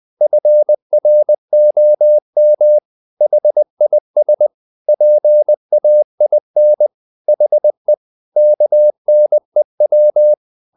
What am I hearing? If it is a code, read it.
Morse code, heard as a keyed tone, about 20 wpm, 600 Hz. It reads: FROM HIS PAIN HE KNEW
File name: the old man and the sea part 2 20wpm